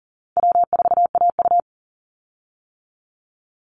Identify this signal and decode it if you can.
Morse code, heard as a keyed tone, about 40 wpm, 700 Hz. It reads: W4AU
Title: Easy Call Signs - 40wpm